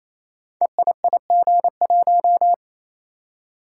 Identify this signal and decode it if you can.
Morse code, heard as a keyed tone, about 28 wpm, 700 Hz. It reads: EIIG1
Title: Code Groups - 5 Letter:Numbers - 28wpm